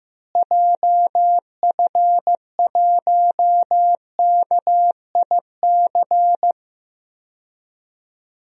Morse code, heard as a keyed tone, about 15 wpm, 700 Hz. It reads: JF1KIC
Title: DXCC-100 Call Signs - 6 Characters - Repeated 3x - 15wpm